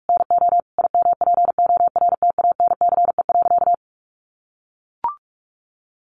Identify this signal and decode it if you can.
Morse code, heard as a keyed tone, about 44 wpm, 700 Hz. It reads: NO IMPORTANCE.
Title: Sentences from Top 1200 Words 45wpm